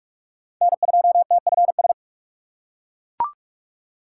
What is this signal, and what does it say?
Morse code, heard as a keyed tone, about 41 wpm, 700 Hz. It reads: N2TFS
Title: USA Call Signs - 5 Characters - 2x Character Spacing - 45wpm